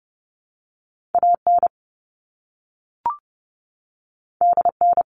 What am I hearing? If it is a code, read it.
Morse code, heard as a keyed tone, about 30 wpm, 700 Hz. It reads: A N D N
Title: All Cut Numbers - 2-Digit 30wpm